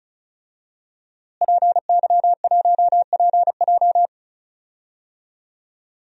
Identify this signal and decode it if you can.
Morse code, heard as a keyed tone, about 36 wpm, 700 Hz. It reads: PY1PJ
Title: DXCC-100 Call Signs - 5 Characters - Repeated 3x - 35wpm